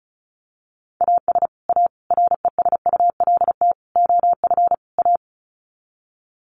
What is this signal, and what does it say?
Morse code, heard as a keyed tone, about 34 wpm, 700 Hz. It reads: AS A RESULT OF A
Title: Sets of 5 Words Encore 35wpm